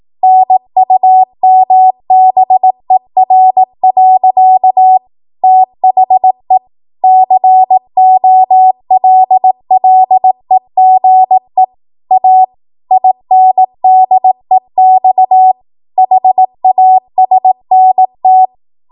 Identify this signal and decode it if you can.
Morse code, heard as a keyed tone, about 18 wpm, 750 Hz. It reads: NUMBER. THE COLLEGE A INDEX HASNT
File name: Archive / 18 WPM / 18WPM